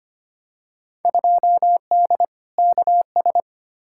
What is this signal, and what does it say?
Morse code, heard as a keyed tone, about 25 wpm, 700 Hz. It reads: 2D KS
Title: Field Day Exchanges Encore - 25wpm